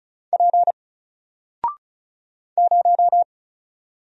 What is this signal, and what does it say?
Morse code, heard as a keyed tone, about 36 wpm, 700 Hz. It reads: P 0
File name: Single-Letter-Number-35wpm